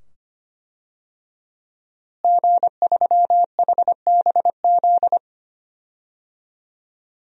Morse code, heard as a keyed tone, about 25 wpm, 700 Hz. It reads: G3HBZ
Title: DXCC-100 Call Signs - 5 Characters - Repeated 2x - 25wpm